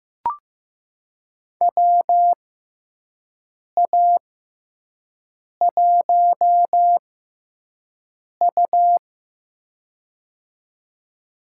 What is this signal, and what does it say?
Morse code, heard as a keyed tone, about 15 wpm, spaced as for 5 wpm, 700 Hz. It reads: WA1U
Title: USA Call Signs - 4 Characters - 6x Character Spacing - 15wpm